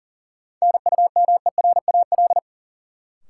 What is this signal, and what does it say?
Morse code, heard as a keyed tone, about 40 wpm, 700 Hz. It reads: NUMERAL
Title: Sentences from Top 1000 Words - Review Part 2 - New Word Drill - 40wpm